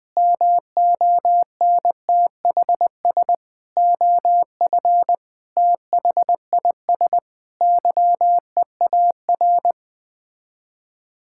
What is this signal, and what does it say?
Morse code, heard as a keyed tone, about 20 wpm, 700 Hz. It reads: MONTHS OF THIS YEAR
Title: Sets of 8 Words - 20wpm